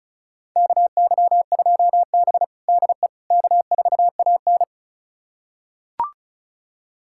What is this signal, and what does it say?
Morse code, heard as a keyed tone, about 36 wpm, 700 Hz. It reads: KY2B DE K4AN
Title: USA Callsign DE Callsign - 4 Characters - 35wpm